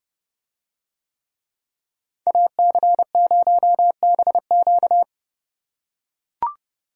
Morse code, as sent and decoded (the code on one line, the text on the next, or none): .- -.-. ----- -... --.-
AC0BQ